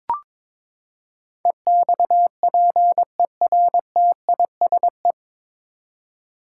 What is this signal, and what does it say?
Morse code, heard as a keyed tone, about 22 wpm, 700 Hz. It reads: EXPERTISE
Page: Morse Code Ninja practice 22wpm